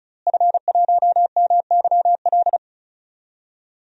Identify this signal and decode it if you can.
Morse code, heard as a keyed tone, about 36 wpm, 700 Hz. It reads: F1MYL
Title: DXCC-100 Call Signs - 5 Characters - Repeated 3x - 35wpm